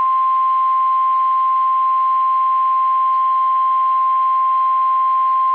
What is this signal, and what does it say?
Emergency Alert System (SAME) tones (the 1050 Hz attention tone). It is heard from 0.9 to 5.0 s.